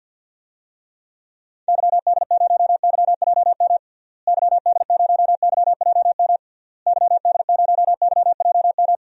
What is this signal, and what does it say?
Morse code, heard as a keyed tone, about 50 wpm, 700 Hz. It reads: YD0YJM YD0YJM YD0YJM
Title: DXCC-100 Call Signs - 6 Characters - Repeated 3x - 50wpm